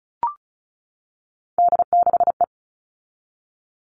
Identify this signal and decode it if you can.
Morse code, heard as a keyed tone, about 34 wpm, 700 Hz. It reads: D6E
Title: All Cut Numbers - 3-Digit 35wpm